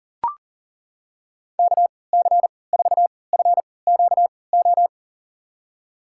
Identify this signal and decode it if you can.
Morse code, heard as a keyed tone, about 39 wpm, spaced as for 22 wpm, 700 Hz. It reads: KC4FQO
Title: USA Call Signs - 6 Characters - 3x Character Spacing - 40wpm